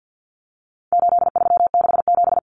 Morse code, heard as a keyed tone, about 50 wpm, 700 Hz. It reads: 8367
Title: Common Cut Numbers - 4-Digit 50wpm